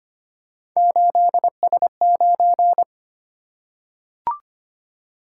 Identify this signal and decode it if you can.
Morse code, heard as a keyed tone, about 25 wpm, 700 Hz. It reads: 8S9
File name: Three Letter-Number Combinations - 25wpm